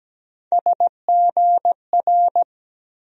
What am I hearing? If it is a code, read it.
Morse code, heard as a keyed tone, about 17 wpm, 700 Hz. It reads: SGR
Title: USA Call Sign Postfixes - 3 Letters 17wpm